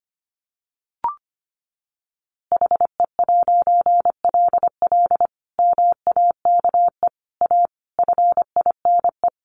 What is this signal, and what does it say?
Morse code, heard as a keyed tone, about 25 wpm, 700 Hz. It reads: HE'LL MAKE A FINE
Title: A Sound Of Thunder 25wpm